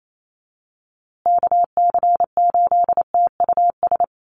Morse code, heard as a keyed tone, about 28 wpm, 700 Hz. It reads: KC8TUS
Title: USA Call Signs Repeated 2x 28wpm